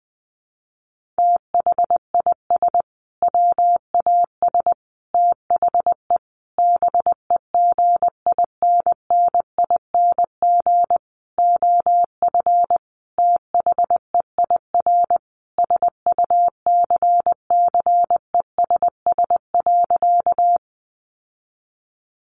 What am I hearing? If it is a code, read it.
Morse code, heard as a keyed tone, about 20 wpm, 700 Hz. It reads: THIS WAS THE BEGINNING OF THEIR SUCCESS.
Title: Sentences from Top 900 Words 20wpm